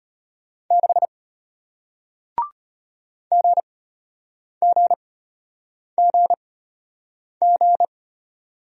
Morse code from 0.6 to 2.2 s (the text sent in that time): 6